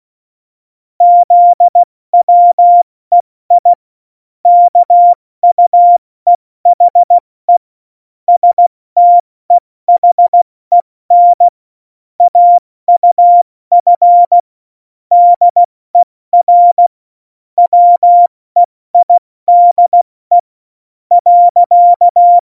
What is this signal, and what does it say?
Morse code, heard as a keyed tone, about 16 wpm, spaced as for 14 wpm, 700 Hz. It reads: ZWEI KUEHE STEHEN AUF DER WEIDE.
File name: Morsekurs / Witze / MP3, Witz 24 April 2018, 70 F80